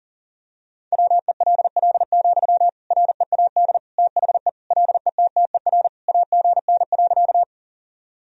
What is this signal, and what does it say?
Morse code, heard as a keyed tone, about 40 wpm, 700 Hz. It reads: WELL, READ THE LETTER AGN.
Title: Sentences from Top 300 Words and Abbreviations 40wpm